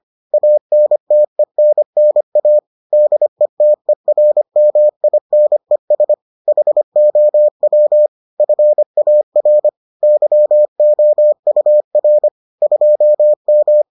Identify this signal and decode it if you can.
Morse code, heard as a keyed tone, about 25 wpm, 600 Hz. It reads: ANTENNA DETERMINES HOW FAR YOUR 2M